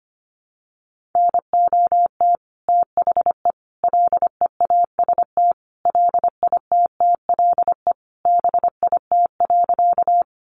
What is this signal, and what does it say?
Morse code, heard as a keyed tone, about 25 wpm, 700 Hz. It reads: NOT THE LEAST LITTLE BIT.
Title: Sentences from Top 900 Words 25wpm